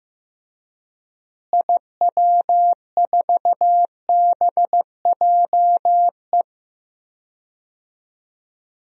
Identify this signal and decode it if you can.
Morse code, heard as a keyed tone, about 15 wpm, 700 Hz. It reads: IW4BJE
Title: DXCC-100 Call Signs - 6 Characters - Repeated 2x - 15wpm